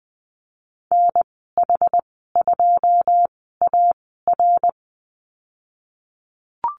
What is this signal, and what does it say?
Morse code, heard as a keyed tone, about 20 wpm, spaced as for 15 wpm, 700 Hz. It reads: NH2AR
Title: 088 Lesson 15g - Number 5 - Call Signs 20wpm